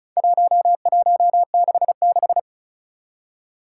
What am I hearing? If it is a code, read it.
Morse code, heard as a keyed tone, about 36 wpm, 700 Hz. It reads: 1166
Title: Four Challenging Letter-Number Combinations 35wpm